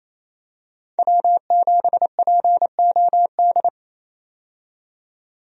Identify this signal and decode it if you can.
Morse code, heard as a keyed tone, about 28 wpm, 700 Hz. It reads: W7POD
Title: USA Call Signs - 5 Characters - 28wpm